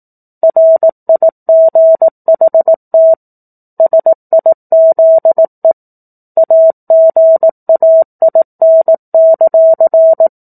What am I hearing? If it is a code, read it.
Morse code, heard as a keyed tone, about 18 wpm, 650 Hz. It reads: RIGHT SIZE AGAIN;
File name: 18 WPM